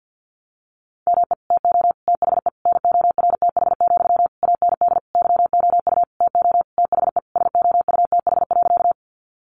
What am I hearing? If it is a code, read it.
Morse code, heard as a keyed tone, about 52 wpm, 700 Hz. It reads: ME TO THE NORTH, AND YOU TO THE SOUTH.
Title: Sentences from Top 400 Words 50wpm